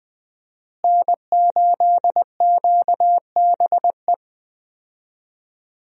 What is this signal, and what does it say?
Morse code, heard as a keyed tone, about 20 wpm, 700 Hz. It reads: N8QBE
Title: USA Call Signs - 5 Characters - Repeated 2x - 20wpm